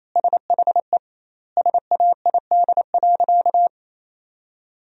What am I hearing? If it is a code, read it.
Morse code, heard as a keyed tone, about 28 wpm, 700 Hz. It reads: SHE SAID.
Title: 2x Word-Spacing - Sentences from Top 400 Words - 28wpm